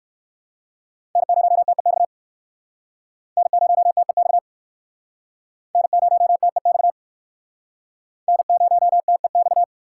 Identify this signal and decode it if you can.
Morse code, heard as a keyed tone, about 55 wpm, 700 Hz. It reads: N0TEX N0TEX N0TEX N0TEX
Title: Speed Racing - USA 1x3 Call Signs 45wpm